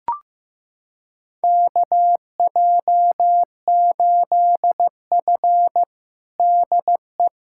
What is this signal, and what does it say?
Morse code, heard as a keyed tone, about 16 wpm, 700 Hz. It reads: KJ8F DE
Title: USA Callsign DE Callsign - 4 Characters - 15wpm